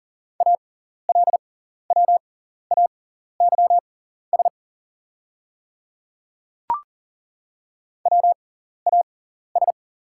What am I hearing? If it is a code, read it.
Morse code, heard as a keyed tone, about 40 wpm, spaced as for 14 wpm, 690 Hz. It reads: ALWAYS WAS
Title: 6x Character Spacing - Top 500 Words - 40wpm